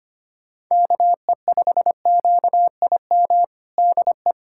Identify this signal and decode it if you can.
Morse code, heard as a keyed tone, about 25 wpm, 700 Hz. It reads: KE5QIM DE
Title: USA Callsign DE Callsign - 6 Characters - 25wpm